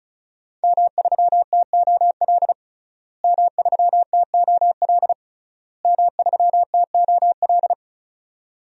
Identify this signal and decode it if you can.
Morse code, heard as a keyed tone, about 34 wpm, 700 Hz. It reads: M3TOL M3TOL M3TOL
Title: DXCC-100 Call Signs - 5 Characters - Repeated 3x - 35wpm